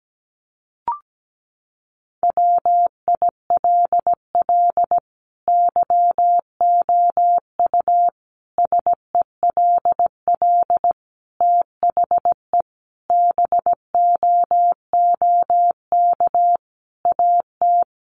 Morse code, heard as a keyed tone, about 17 wpm, 700 Hz. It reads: WILL YOU SELL THE BOOK AT
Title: Sentences from Top 1200 Words 17wpm